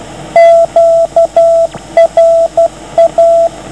Morse code, heard as a keyed tone, about 12 wpm, 660 Hz, with radio noise. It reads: QRA